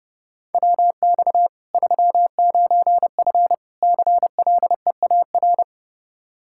Morse code, heard as a keyed tone, about 32 wpm, 700 Hz. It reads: WX 39F CLEAR
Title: Practice QSO - Exchange 2 - Info - 30wpm